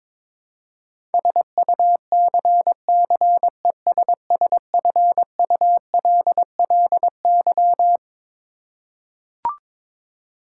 Morse code, heard as a keyed tone, about 22 wpm, 700 Hz. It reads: SUCCESSFULLY
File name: Twelve Letter Words - 22wpm